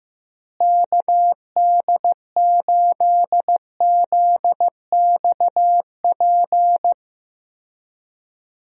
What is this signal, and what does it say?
Morse code, heard as a keyed tone, about 15 wpm, 700 Hz. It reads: KD8ZXP
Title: USA Call Signs - 6 Characters - Repeated 3x - 15wpm